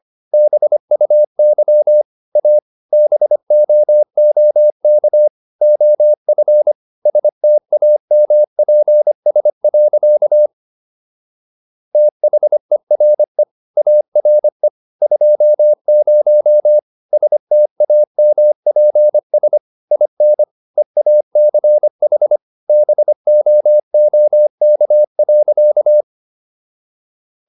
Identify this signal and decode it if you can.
Morse code, heard as a keyed tone, about 25 wpm, 600 Hz. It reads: BUY A BOOK OF STAMPS. THERE ARE 20 STAMPS IN EACH BOOK.